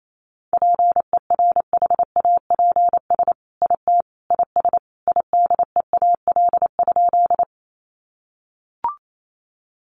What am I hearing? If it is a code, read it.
Morse code, heard as a keyed tone, about 28 wpm, 700 Hz. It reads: PERHAPS IT IS IDEAL?
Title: Sentences from Top 1400 Words 28wpm